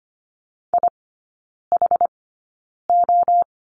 Morse code, heard as a keyed tone, about 25 wpm, spaced as for 9 wpm, 700 Hz. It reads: IHO
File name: course / 25:10wpm / 093 Lesson 16e - Letter c - Three Characters 25wpm